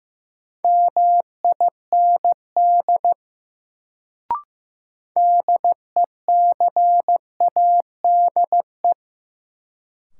Morse code, heard as a keyed tone, about 16 wpm, 700 Hz. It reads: MIND DECADE
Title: Sets of 5 Words - New Word Drill - 15wpm